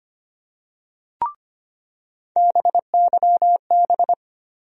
Morse code, heard as a keyed tone, about 25 wpm, 700 Hz. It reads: BYB
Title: USA Call Sign Postfixes - 3 Letters 25wpm